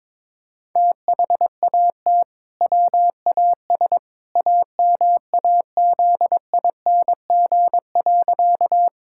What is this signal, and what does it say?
Morse code, heard as a keyed tone, about 22 wpm, 700 Hz. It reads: THAT WAS AMAZING.
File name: practice / 22wpm / Sentences with 2-Word Phrases - 22wpm